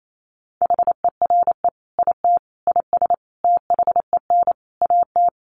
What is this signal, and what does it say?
Morse code, heard as a keyed tone, about 28 wpm, 700 Hz. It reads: HERE IT IS THEN AT
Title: Sentences from Top 1500 Words 28wpm